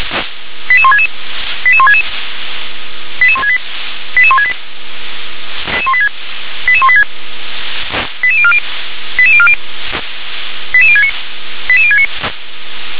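Five-tone selective call (ZVEI), heard as quick runs of five tones, tones of about 70 ms. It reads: 88166 88166 88168 88168 88186 88040 88040 88069 88069
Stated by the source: ZVEI five-tone selective call received by SDR.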